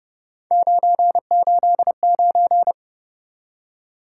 Morse code, heard as a keyed tone, about 29 wpm, 700 Hz. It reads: 989